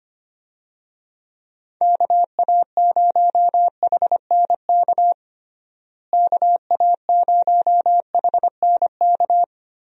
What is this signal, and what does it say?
Morse code, heard as a keyed tone, about 25 wpm, 700 Hz. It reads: KA0HNK KA0HNK
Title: USA Call Signs - 6 Characters - Repeated 3x - 25wpm